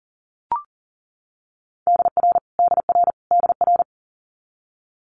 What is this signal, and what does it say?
Morse code, heard as a keyed tone, about 39 wpm, 700 Hz. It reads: DR DR DR
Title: Multi-letter Phonemes Repeated 3x 40wpm